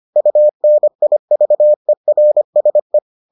Morse code, heard as a keyed tone, about 25 wpm, 600 Hz. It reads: UNIVERSE